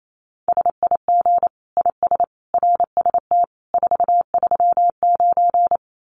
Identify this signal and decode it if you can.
Morse code, heard as a keyed tone, about 28 wpm, 700 Hz. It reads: SIG IS RST 439
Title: Common QSO Phrases 28wpm